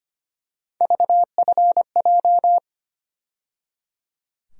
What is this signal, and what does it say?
Morse code, heard as a keyed tone, about 25 wpm, 700 Hz. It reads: VFJ
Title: USA Call Sign Postfixes - 3 Letters 25wpm